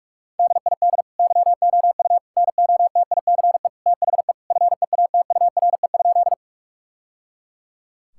Morse code, heard as a keyed tone, about 41 wpm, 700 Hz. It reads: DID YOU NOTICE THE FEATURE?
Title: Sentences from Top 1000 Words - Review Part 5 - 45wpm